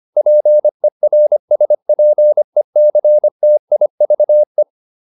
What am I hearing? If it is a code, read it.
Morse code, heard as a keyed tone, about 25 wpm, 600 Hz. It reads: PERSPECTIVE